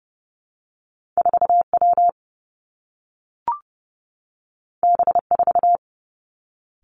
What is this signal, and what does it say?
Morse code, heard as a keyed tone, about 30 wpm, 700 Hz. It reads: 4 W B 4
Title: Two Challenging Letter-Number Combinations 30wpm